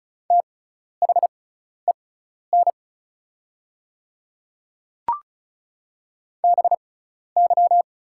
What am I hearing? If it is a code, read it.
Morse code, heard as a keyed tone, about 36 wpm, spaced as for 12 wpm, 700 Hz. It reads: THEN BY
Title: 6x Character Spacing - Top 100 Words - 35wpm